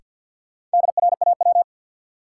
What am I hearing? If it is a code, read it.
Morse code, heard as a keyed tone, about 52 wpm, 700 Hz. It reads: DRAW
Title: Sentences from Top 300 Words - New Word Drill - 50wpm